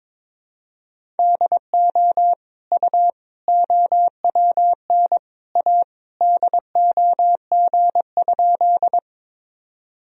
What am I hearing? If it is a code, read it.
Morse code, heard as a keyed tone, about 22 wpm, 700 Hz. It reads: DO U OWN A DOG?
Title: Sentences from Top 400 Words and Abbreviations 22wpm